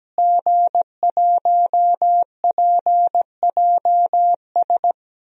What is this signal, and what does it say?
Morse code, heard as a keyed tone, about 17 wpm, 700 Hz. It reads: G1PJS
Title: DXCC-100 Call Signs - 5 Characters - Repeated 3x - 17wpm